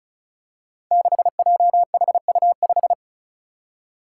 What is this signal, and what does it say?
Morse code, heard as a keyed tone, about 36 wpm, 700 Hz. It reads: 6JHU5
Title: Five Challenging Letter-Number Combinations 35wpm